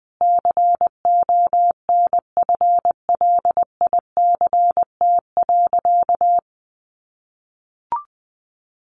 Morse code, heard as a keyed tone, about 20 wpm, 700 Hz. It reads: CONFLICT.
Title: Sentences from Top 1300 Words 20wpm